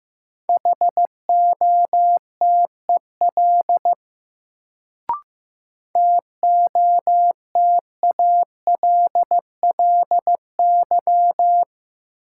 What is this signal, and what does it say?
Morse code, heard as a keyed tone, about 15 wpm, 700 Hz. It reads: HOTEL TOTALLY
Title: Sentences from Top 1600 Words - New Word Drill Alt - 15wpm